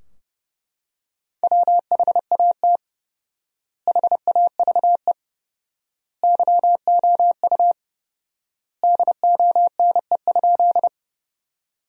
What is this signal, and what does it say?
Morse code, heard as a keyed tone, about 32 wpm, 700 Hz. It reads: WHAT HAVE YOU DONE?
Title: 4x Word-Spacing - Sentences from Top 500 Words - 30wpm